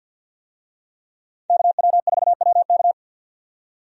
Morse code, heard as a keyed tone, about 51 wpm, 700 Hz. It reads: KW4WK
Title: USA Call Signs - 5 Characters - 50wpm